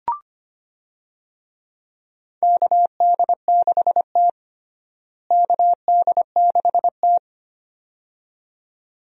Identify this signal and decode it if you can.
Morse code, heard as a keyed tone, about 25 wpm, 700 Hz. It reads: KD6T KD6T
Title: USA Call Signs - 4 Characters - Repeated 2x - 25wpm